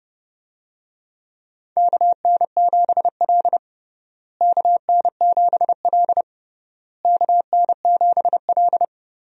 Morse code, heard as a keyed tone, about 30 wpm, 700 Hz. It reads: KN7L KN7L KN7L
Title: USA Call Signs - 4 Characters - Repeated 3x - 30wpm